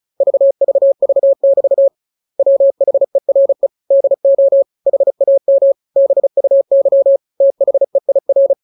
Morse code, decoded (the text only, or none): VVV<BT> WHERE DO HAM BUY THEIR